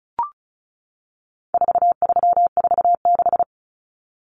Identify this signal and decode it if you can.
Morse code, heard as a keyed tone, about 34 wpm, 700 Hz. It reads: 4346
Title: Numbers - 4-Digit 35wpm